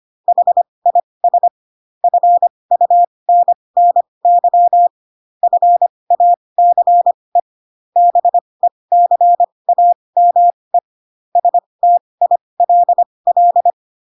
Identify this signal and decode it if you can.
Morse code, heard as a keyed tone, about 25 wpm, spaced as for 20 wpm, 700 Hz. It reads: HIS FUNNY FACE BECAME STILL
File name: Pinocchio / Pinocchio ch1 15 25 ik1hge